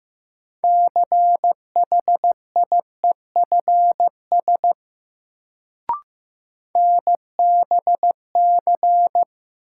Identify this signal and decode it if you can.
Morse code, heard as a keyed tone, about 15 wpm, 700 Hz. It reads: CHIEFS NBC
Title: Sets of 5 Words Encore - New Word Drill - 15wpm